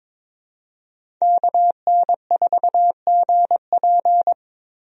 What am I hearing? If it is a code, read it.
Morse code, heard as a keyed tone, about 22 wpm, 700 Hz. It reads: KN4GP
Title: USA Call Signs - 5 Characters - 22wpm